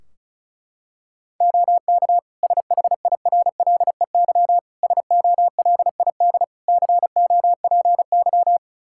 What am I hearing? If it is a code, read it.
Morse code, heard as a keyed tone, about 36 wpm, 700 Hz. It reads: OK SHIRLEY SOLID COPY
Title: Practice QSO - Exchange 3 - More Info - 35wpm